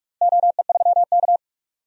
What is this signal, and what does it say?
Morse code, heard as a keyed tone, about 43 wpm, 700 Hz. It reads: OE3K
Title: DXCC-100 Call Signs - 4 Characters - Repeated 3x - 45wpm